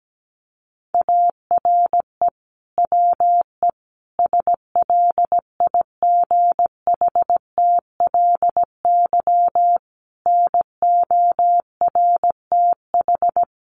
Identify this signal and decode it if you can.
Morse code, heard as a keyed tone, about 17 wpm, 700 Hz. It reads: ARE WE SLIGHTLY NORTH
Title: Sentences from Top 1300 Words 17wpm